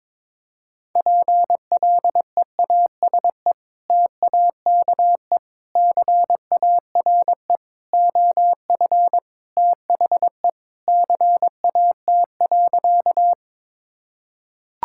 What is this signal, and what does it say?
Morse code, heard as a keyed tone, about 21 wpm, 700 Hz. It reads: PLEASE TAKE CARE OF THE CAT.
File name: practice / 22wpm / Sentences with 3-Word Phrases - 22wpm